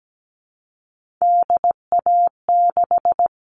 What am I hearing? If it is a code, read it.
Morse code, heard as a keyed tone, about 17 wpm, 700 Hz. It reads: DA6
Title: All Cut Numbers - 3-Digit 17wpm